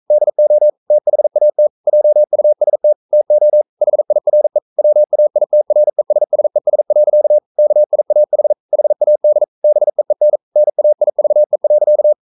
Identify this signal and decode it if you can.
Morse code, heard as a keyed tone, about 44 wpm, 600 Hz. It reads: DO THAT JUST TO HIRE WAITRESSES. KIAH HAD BEEN NAIVE.
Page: Romanen Never af Ken Follet på engelsk.